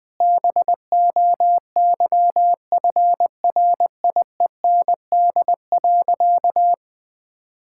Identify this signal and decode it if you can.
Morse code, heard as a keyed tone, about 20 wpm, 700 Hz. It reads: BOYFRIEND.
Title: Sentences from Top 1000 Words - Review Part 5 - 20wpm